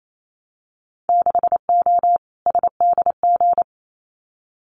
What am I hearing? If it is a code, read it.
Morse code, heard as a keyed tone, about 28 wpm, 700 Hz. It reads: 6O SDG
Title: Winter Field Day Exchanges Encore 28wpm